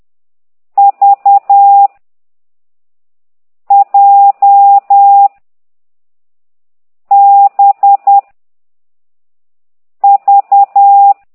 Morse code, heard as a keyed tone, about 10 wpm, 800 Hz. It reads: V J B V